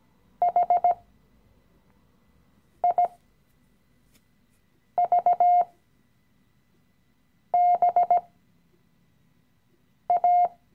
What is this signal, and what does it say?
Morse code, heard as a keyed tone, about 18 wpm, spaced as for 4 wpm, 710 Hz. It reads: HIVBA
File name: CW-Online-Kurs / Anfänger / Lektion 016